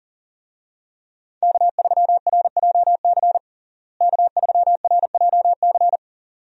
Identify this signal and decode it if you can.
Morse code, heard as a keyed tone, about 37 wpm, 700 Hz. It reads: K3RJC K3RJC
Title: USA Call Signs - 5 Characters - Repeated 2x - 40wpm